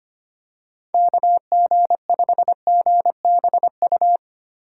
Morse code, heard as a keyed tone, about 25 wpm, 700 Hz. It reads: KG5GBU
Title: USA Call Signs - 2x Character Spacing - 25wpm